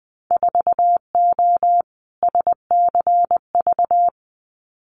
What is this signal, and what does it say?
Morse code, heard as a keyed tone, about 20 wpm, 700 Hz. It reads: 4O SCV
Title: Winter Field Day Exchanges 20wpm